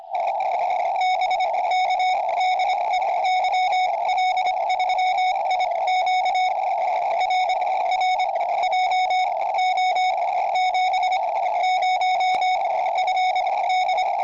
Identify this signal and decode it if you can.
Morse code, heard as a keyed tone, about 25 wpm, spaced as for 18 wpm, 2200 Hz, with radio noise. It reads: BK DE YL3IQ R R JO70FD
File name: tropo / yl3iq